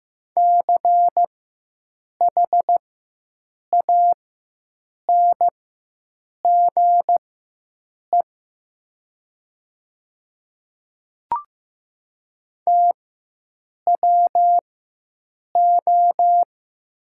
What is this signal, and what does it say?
Morse code, heard as a keyed tone, about 15 wpm, spaced as for 7 wpm, 700 Hz. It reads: CHANGE TWO
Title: 4x Character Spacing - Top 300 Words - 15wpm